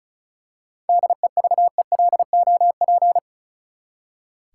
Morse code, heard as a keyed tone, about 34 wpm, 700 Hz. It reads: DEVELOP
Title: Seven Letter Words - 35wpm